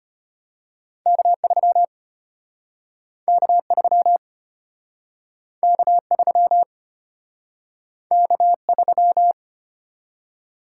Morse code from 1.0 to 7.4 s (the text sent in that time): K3 K3 K3